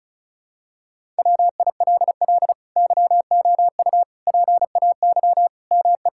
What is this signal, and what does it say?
Morse code, heard as a keyed tone, about 36 wpm, 700 Hz. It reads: WILL YOU PAY ME
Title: Sentences from Top 1600 Words 35wpm